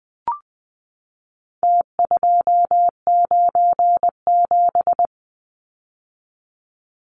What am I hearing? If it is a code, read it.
Morse code, heard as a keyed tone, about 20 wpm, 700 Hz. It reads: T297
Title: Numbers - Contest Style - 4-Digit 20wpm